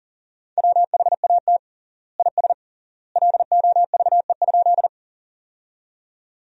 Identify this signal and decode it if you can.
Morse code, heard as a keyed tone, about 40 wpm, 700 Hz. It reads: WHAT IS LOVE?
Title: 3x Word-Spacing - Sentences from Top 400 Words - 40wpm